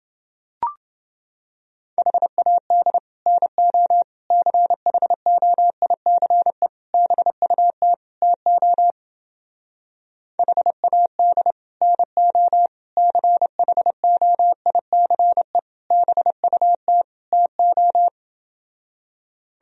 Morse code, heard as a keyed tone, about 29 wpm, 700 Hz. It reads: HAD NO CHOICE BUT TO HAD NO CHOICE BUT TO